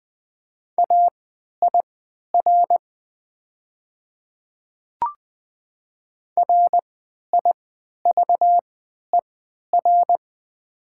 Morse code, heard as a keyed tone, about 20 wpm, spaced as for 11 wpm, 700 Hz. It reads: AIR RIVER
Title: 3x Character Spacing - Top 300 Words - 20wpm